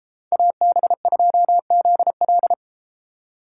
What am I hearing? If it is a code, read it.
Morse code, heard as a keyed tone, about 33 wpm, 700 Hz. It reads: AB2ZL